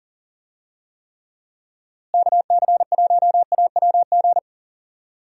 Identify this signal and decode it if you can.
Morse code, heard as a keyed tone, about 40 wpm, 700 Hz. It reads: KC1AWG
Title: USA Call Signs - 6 Characters - Repeated 3x - 40wpm